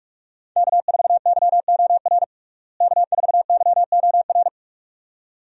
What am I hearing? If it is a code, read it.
Morse code, heard as a keyed tone, about 45 wpm, 700 Hz. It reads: K4YOR K4YOR
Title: USA Call Signs - 5 Characters - Repeated 2x - 45wpm